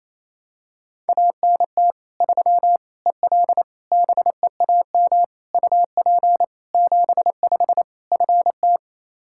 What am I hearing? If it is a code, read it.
Morse code, heard as a keyed tone, about 28 wpm, 700 Hz. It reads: ANT 3 EL BEAM UP 75 FT
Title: Practice QSO - Exchange 2 - Info - 28wpm